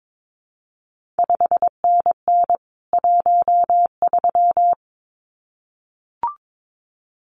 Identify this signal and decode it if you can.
Morse code, heard as a keyed tone, about 22 wpm, 700 Hz. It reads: 5NN 13
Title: CQ WW DX Contest Exchange - 22wpm